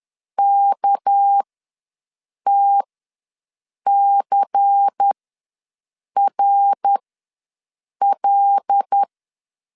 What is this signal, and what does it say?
Morse code, heard as a keyed tone, about 10 wpm, spaced as for 6 wpm, 800 Hz. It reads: KTCRL
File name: LEZIONI / 6 PKX0 / 6 PK ws 10wpm 3745